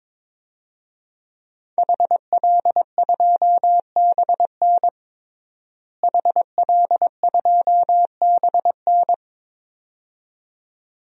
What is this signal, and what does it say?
Morse code, heard as a keyed tone, about 22 wpm, 700 Hz. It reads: HL2BN HL2BN
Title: DXCC-100 Call Signs - 5 Characters - Repeated 2x - 22wpm